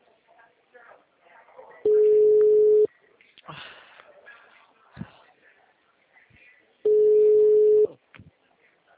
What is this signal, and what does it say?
Ringback tone (Europe, 425 Hz), with speech.
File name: out- -2011- - - .0